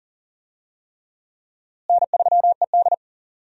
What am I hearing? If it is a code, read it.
Morse code, heard as a keyed tone, about 37 wpm, 700 Hz. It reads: N3ED
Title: USA Call Signs - 4 Characters - Repeated 3x - 40wpm